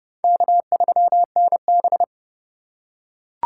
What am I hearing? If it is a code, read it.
Morse code, heard as a keyed tone, about 30 wpm, 700 Hz. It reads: K3NB
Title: USA Call Signs - 4 Characters - 2x Character Spacing - 30wpm